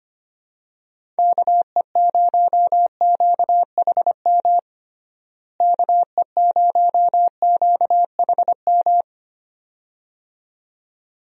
Morse code, heard as a keyed tone, about 25 wpm, 700 Hz. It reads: KE0QHM KE0QHM
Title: USA Call Signs - 6 Characters - Repeated 2x - 25wpm